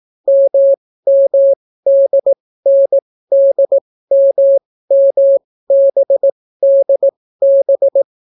Morse code, heard as a keyed tone, about 18 wpm, spaced as for 14 wpm, 550 Hz. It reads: MMDNDMMBDB